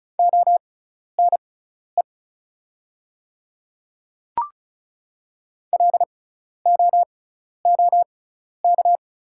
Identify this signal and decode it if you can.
Morse code, heard as a keyed tone, about 36 wpm, spaced as for 12 wpm, 700 Hz. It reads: ONE LOOK
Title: 6x Character Spacing - Top 100 Words - 35wpm